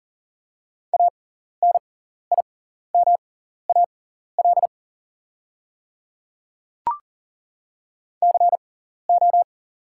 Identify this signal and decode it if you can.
Morse code, heard as a keyed tone, about 40 wpm, spaced as for 14 wpm, 690 Hz. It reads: ANIMAL CO
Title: 6x Character Spacing - Top 200 Words - 40wpm